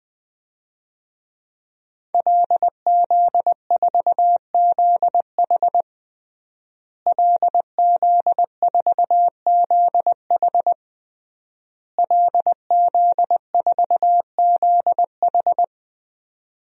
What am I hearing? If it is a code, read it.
Morse code, heard as a keyed tone, about 20 wpm, 700 Hz. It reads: LZ4ZH LZ4ZH LZ4ZH
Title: DXCC-100 Call Signs - 5 Characters - Repeated 3x - 20wpm